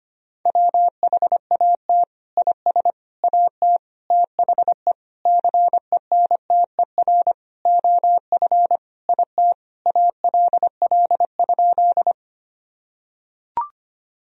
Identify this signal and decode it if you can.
Morse code, heard as a keyed tone, about 25 wpm, 700 Hz. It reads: WHAT IS AT THE CENTER OF IT ALL?
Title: Sentences from Top 1000 Words - Review Part 2 - 25wpm